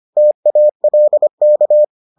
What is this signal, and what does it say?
Morse code, heard as a keyed tone, about 25 wpm, 600 Hz. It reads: TALK